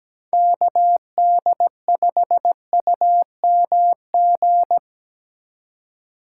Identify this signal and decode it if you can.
Morse code, heard as a keyed tone, about 19 wpm, 700 Hz. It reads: KD5UMG
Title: Speed Racing - USA Call Signs 17wpm